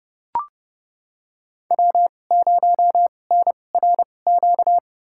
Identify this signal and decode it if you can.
Morse code, heard as a keyed tone, about 30 wpm, spaced as for 22 wpm, 700 Hz. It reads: W0NRQ
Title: USA Call Signs - 5 Characters - 2x Character Spacing - 30wpm